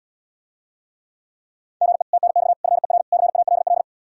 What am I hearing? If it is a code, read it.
Morse code, heard as a keyed tone, about 80 wpm, 700 Hz. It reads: GE TNX FER CALL